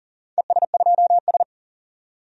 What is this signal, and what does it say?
Morse code, heard as a keyed tone, about 40 wpm, 700 Hz. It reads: ES2S
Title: DXCC-100 Call Signs - 4 Characters - 40wpm